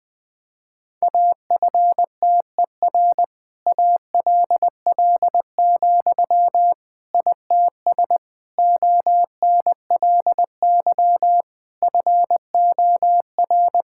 Morse code, heard as a keyed tone, about 21 wpm, 700 Hz. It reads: AFTER ALL, ITS ONLY FOR
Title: Sentences from Top 1000 Words - Review Part 5 - 20wpm